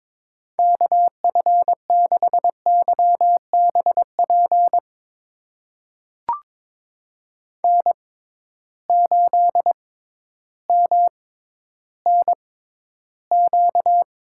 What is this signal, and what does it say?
Morse code, heard as a keyed tone, about 22 wpm, spaced as for 12 wpm, 700 Hz. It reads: KF6YBP N8MNQ
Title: USA Call Signs - 6x Character Spacing - 22wpm